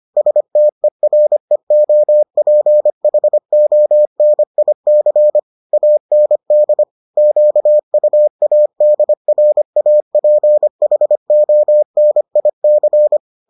Morse code, heard as a keyed tone, about 25 wpm, 600 Hz. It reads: STEREOPHONIC AND QUADRAPHONIC